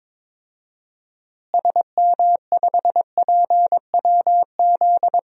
Morse code, heard as a keyed tone, about 22 wpm, 700 Hz. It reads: SM5PWZ
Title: DXCC-100 Call Signs - 6 Characters - Repeated 3x - 22wpm